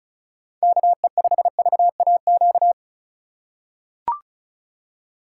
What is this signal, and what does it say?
Morse code, heard as a keyed tone, about 36 wpm, 700 Hz. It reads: KE5VAQ
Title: USA Call Signs - 6 Characters - 8x Character Spacing - 35wpm